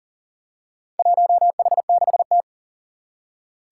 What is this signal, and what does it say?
Morse code, heard as a keyed tone, about 38 wpm, 700 Hz. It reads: 1H6T
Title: Four Challenging Letter-Number Combinations 40wpm